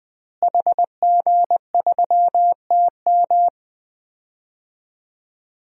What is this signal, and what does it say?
Morse code, heard as a keyed tone, about 20 wpm, 700 Hz. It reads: HG3TM
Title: DXCC-100 Call Signs - 5 Characters - Repeated 2x - 20wpm